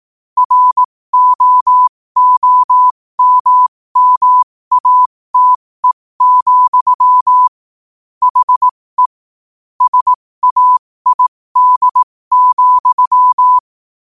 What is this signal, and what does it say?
Morse code, heard as a keyed tone, about 18 wpm, spaced as for 15 wpm, 1000 Hz. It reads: ROOMMATE, HE SAID,
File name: qotdfiles / 15wpm